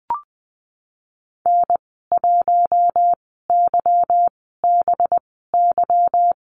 Morse code, heard as a keyed tone, about 20 wpm, spaced as for 15 wpm, 700 Hz. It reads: N1YBY
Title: 193 Lesson 31g - Letter v - Call Signs 20wpm